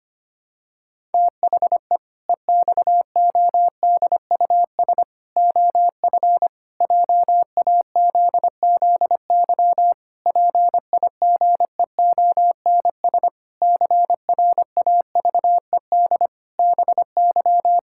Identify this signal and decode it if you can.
Morse code, heard as a keyed tone, about 25 wpm, 700 Hz. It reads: THE EXODUS OF JAZZY PIGEONS CRAVED BY
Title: Pangram Challenge v2 - 25wpm